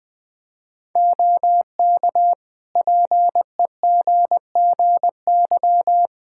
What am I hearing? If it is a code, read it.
Morse code, heard as a keyed tone, about 20 wpm, 700 Hz. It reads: OK PEGGY
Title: Practice QSO - Exchange 3 - More Info - 20wpm